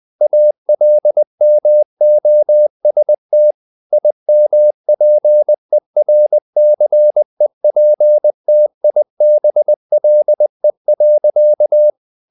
Morse code, heard as a keyed tone, about 20 wpm, 600 Hz. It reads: ALMOST IMPERCEPTIBLE.
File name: the old man and the sea / the old man and the sea part 1 20wpm